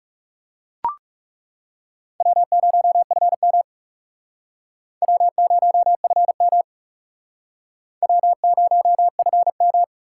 Morse code, heard as a keyed tone, about 38 wpm, 700 Hz. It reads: W0FM W0FM W0FM